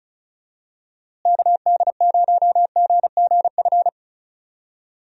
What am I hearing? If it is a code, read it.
Morse code, heard as a keyed tone, about 34 wpm, 700 Hz. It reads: KD0GGF
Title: USA Call Signs - 6 Characters - Repeated 2x - 35wpm